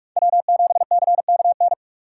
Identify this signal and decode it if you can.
Morse code, heard as a keyed tone, about 47 wpm, 700 Hz. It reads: W7CKN
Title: USA Call Signs - 4x Character Spacing - 45wpm